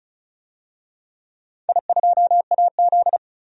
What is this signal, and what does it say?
Morse code, heard as a keyed tone, about 34 wpm, 700 Hz. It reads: I2AZ
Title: DXCC-100 Call Signs - 4 Characters - 35wpm